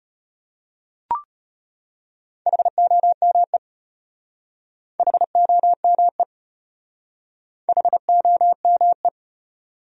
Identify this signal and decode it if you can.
Morse code, heard as a keyed tone, about 33 wpm, 700 Hz. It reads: HOME HOME HOME